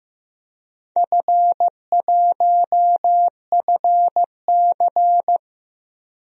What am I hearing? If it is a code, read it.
Morse code, heard as a keyed tone, about 15 wpm, 700 Hz. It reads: F1FC
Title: DXCC-100 Call Signs - 4 Characters - 15wpm